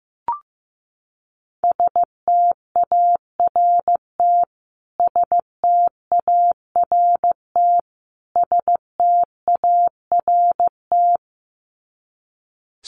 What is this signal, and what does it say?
Morse code, heard as a keyed tone, about 15 wpm, 700 Hz. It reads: START START START
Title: Top 500 Words Repeated 15wpm